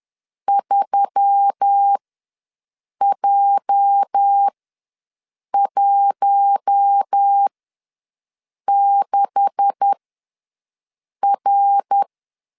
Morse code, heard as a keyed tone, about 10 wpm, spaced as for 6 wpm, 800 Hz. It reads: 3J16R
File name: LEZIONI / 9 2867 ws 10wpm 4215